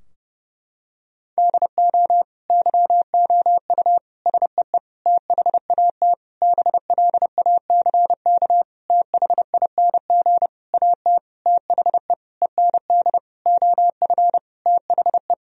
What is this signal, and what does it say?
Morse code, heard as a keyed tone, about 30 wpm, 700 Hz. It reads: DO YOU SEE THAT BLACK THING AT THE END OF THE
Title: Sentences from Top 1000 Words - Review Part 2 - 30wpm